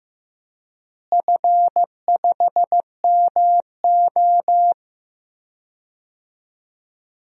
Morse code, heard as a keyed tone, about 15 wpm, 700 Hz. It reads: F5MO
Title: DXCC-100 Call Signs - 4 Characters - Repeated 2x - 15wpm